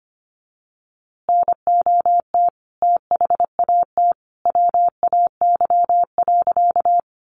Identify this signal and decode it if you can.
Morse code, heard as a keyed tone, about 25 wpm, 700 Hz. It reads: NOT THAT WAY.
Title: A Sound Of Thunder 25wpm